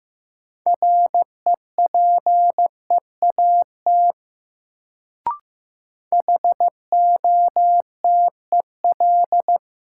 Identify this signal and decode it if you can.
Morse code, heard as a keyed tone, about 15 wpm, 700 Hz. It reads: REPEAT HOTEL
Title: Sentences from Top 1600 Words - New Word Drill - 15wpm